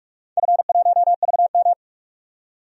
Morse code, heard as a keyed tone, about 43 wpm, 700 Hz. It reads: F1VM
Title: DXCC-100 Call Signs - 4 Characters - 45wpm